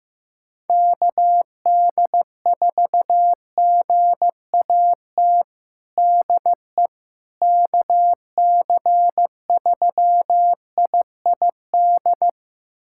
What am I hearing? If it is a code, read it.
Morse code, heard as a keyed tone, about 15 wpm, 700 Hz. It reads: KD4GAT DE KC3IID
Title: USA Callsign DE Callsign - 6 Characters - 15wpm